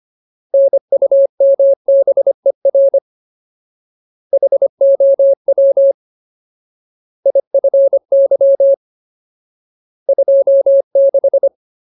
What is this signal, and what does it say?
Morse code, heard as a keyed tone, about 25 wpm, 550 Hz. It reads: NUMBER HOW IFY 26
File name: Ultimate Word List / 25WPM Ultimate